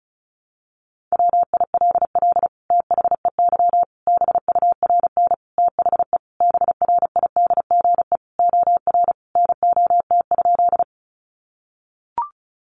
Morse code, heard as a keyed tone, about 34 wpm, 700 Hz. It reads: WILL THEY BURN THE BRIDGE OR NOT?
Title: Sentences from Top 1300 Words 35wpm